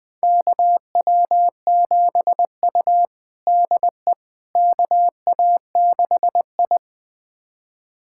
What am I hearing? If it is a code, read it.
Morse code, heard as a keyed tone, about 20 wpm, 700 Hz. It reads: KW7U DE KA6I
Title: USA Callsign DE Callsign - 4 Characters - 20wpm